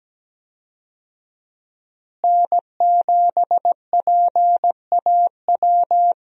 Morse code, heard as a keyed tone, about 17 wpm, 700 Hz. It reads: N7PAW
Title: USA Call Signs - 5 Characters - Repeated 3x - 17wpm